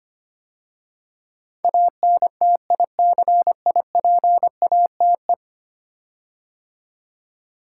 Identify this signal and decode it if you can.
Morse code, heard as a keyed tone, about 25 wpm, 700 Hz. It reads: ANTICIPATE
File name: 10 Letter Words - First 3 Letters Given - 25wpm